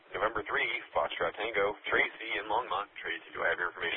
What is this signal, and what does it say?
Emergency Alert System (SAME) tones, heard as harsh buzzing bursts, with speech. It is heard from 0.5 to 3.7 s.